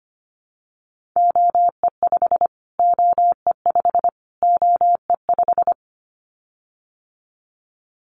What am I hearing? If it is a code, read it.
Morse code, heard as a keyed tone, about 25 wpm, 700 Hz. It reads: OE5 OE5 OE5
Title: Callsign Prefixes for Top 25 DXCC - Repeated 3x - 25wpm